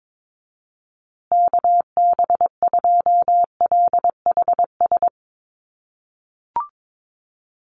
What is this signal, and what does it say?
Morse code, heard as a keyed tone, about 22 wpm, 700 Hz. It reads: KB2LHS
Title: USA Call Signs Repeated 3x 22wpm